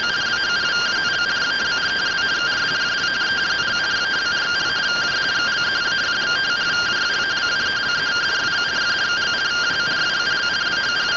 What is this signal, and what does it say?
Two-tone teleprinter signal, about 170 Hz shift.